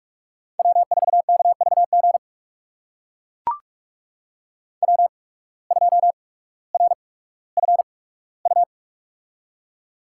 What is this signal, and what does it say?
Morse code, heard as a keyed tone, about 44 wpm, 690 Hz. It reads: W 4 K V G W 2 R F U
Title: USA Call Signs - 8x Character Spacing - 45wpm